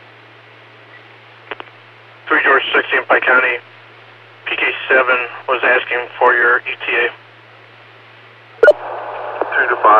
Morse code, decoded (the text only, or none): none